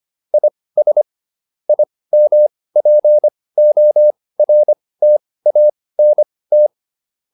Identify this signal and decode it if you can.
Morse code, heard as a keyed tone, about 25 wpm, spaced as for 18 wpm, 600 Hz. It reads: IS IMPORTANT